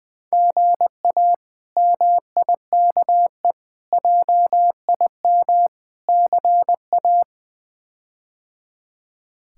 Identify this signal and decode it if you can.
Morse code, heard as a keyed tone, about 21 wpm, 700 Hz. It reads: GA MIKE JIM CA
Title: K1USN SST QSO Practice - 20wpm